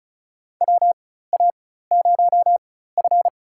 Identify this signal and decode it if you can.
Morse code, heard as a keyed tone, about 34 wpm, spaced as for 16 wpm, 700 Hz. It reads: WA0F
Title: USA Call Signs - 4 Characters - 4x Character Spacing - 35wpm